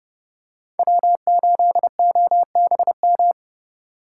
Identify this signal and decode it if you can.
Morse code, heard as a keyed tone, about 31 wpm, 700 Hz. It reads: W8OBM